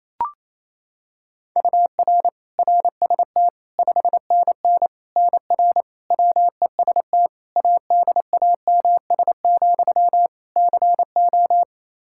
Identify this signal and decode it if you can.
Morse code, heard as a keyed tone, about 28 wpm, 700 Hz. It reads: UR RST 5NN NR WEST ADAMS, CO
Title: Practice QSO - Exchange 1 - Beginning - 28wpm